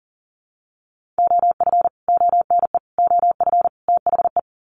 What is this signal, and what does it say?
Morse code, heard as a keyed tone, about 40 wpm, 700 Hz. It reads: OF ONE OF THE
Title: Sets of 4 Words 40wpm